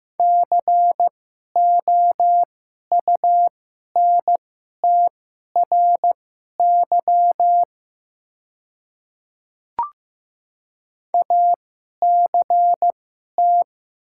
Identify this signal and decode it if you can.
Morse code, heard as a keyed tone, about 15 wpm, spaced as for 11 wpm, 700 Hz. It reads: COUNTRY ACT
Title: 2x Character Spacing - Top 300 Words - 15wpm